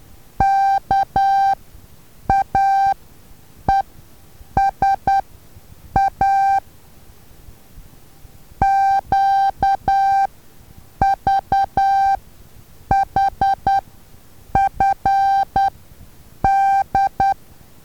Morse code, heard as a keyed tone, about 10 wpm, spaced as for 7 wpm, 780 Hz. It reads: KAESA QVHFD
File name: Taviro HA6VA B